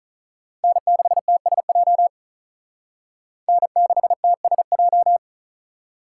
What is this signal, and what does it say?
Morse code, heard as a keyed tone, about 46 wpm, 700 Hz. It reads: N6TSJ N6TSJ